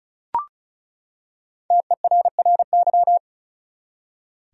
Morse code, heard as a keyed tone, about 36 wpm, 700 Hz. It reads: TERRY
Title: USA Common Names - Max 5 Letters - 35wpm